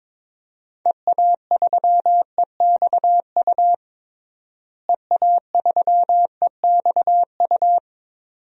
Morse code, heard as a keyed tone, about 22 wpm, 700 Hz. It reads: EA3EXU EA3EXU
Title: DXCC-100 Call Signs - 6 Characters - Repeated 2x - 22wpm